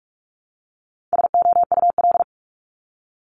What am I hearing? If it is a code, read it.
Morse code, heard as a keyed tone, about 46 wpm, 700 Hz. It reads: SOUL
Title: Multi-letter Phonemes Repeated 3x 45wpm